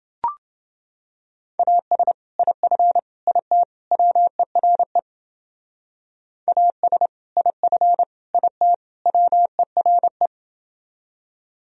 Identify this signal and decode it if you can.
Morse code, heard as a keyed tone, about 29 wpm, 700 Hz. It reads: AS IF IT WERE AS IF IT WERE